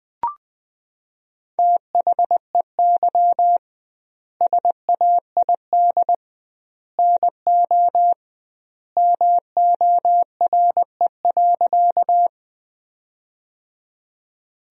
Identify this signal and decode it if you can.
Morse code, heard as a keyed tone, about 21 wpm, 700 Hz. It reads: THEY SAID NO MORE.
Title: 2x Word-Spacing - Sentences from Top 100 Words - 20wpm